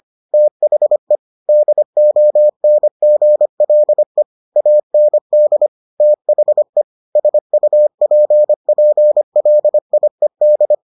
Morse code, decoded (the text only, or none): THE DONGLE AND THE SUPPLIED